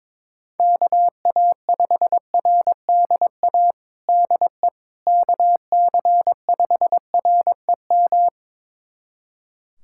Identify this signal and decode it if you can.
Morse code, heard as a keyed tone, about 22 wpm, 700 Hz. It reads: KA5RDA DE KC5REM
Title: USA Callsign DE Callsign 22wpm